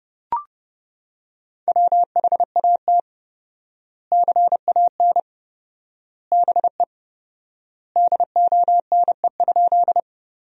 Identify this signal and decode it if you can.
Morse code, heard as a keyed tone, about 30 wpm, 700 Hz. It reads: WHAT CAN BE DONE?
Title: 4x Word-Spacing - Sentences from Top 500 Words - 30wpm